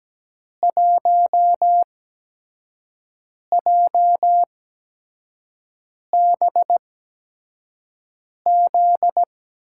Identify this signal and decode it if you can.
Morse code, heard as a keyed tone, about 17 wpm, spaced as for 5 wpm, 700 Hz. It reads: 1JBZ
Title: USA Call Signs - 6 Characters - 8x Character Spacing - 17wpm